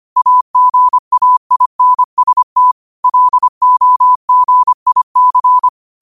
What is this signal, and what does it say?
Morse code, heard as a keyed tone, about 25 wpm, 1000 Hz. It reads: AGAINST LOGIC